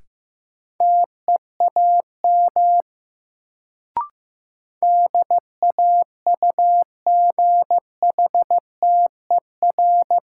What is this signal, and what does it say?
Morse code, heard as a keyed tone, about 15 wpm, 700 Hz. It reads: TEAM DAUGHTER
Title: Sentences from Top 1500 Words - New Word Drill Alt - 15wpm